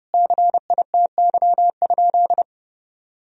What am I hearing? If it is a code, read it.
Morse code, heard as a keyed tone, about 32 wpm, 700 Hz. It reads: CITY?
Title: Sentences from Top 1000 Words - Review Part 2 - 30wpm